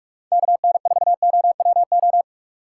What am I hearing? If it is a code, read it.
Morse code, heard as a keyed tone, about 45 wpm, 700 Hz. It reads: KN4OWO